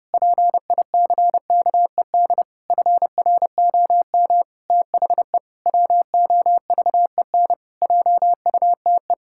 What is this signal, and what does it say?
Morse code, heard as a keyed tone, about 30 wpm, 700 Hz. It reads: PICKED FROM THE WOVEN JUTE
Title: Pangram Challenge v2 - 30wpm